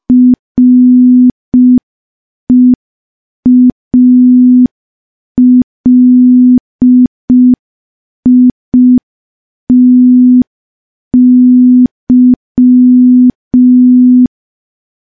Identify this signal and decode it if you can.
Morse code, heard as a keyed tone, about 5 wpm, 250 Hz. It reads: REALITY
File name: morse / morse-training-05WPM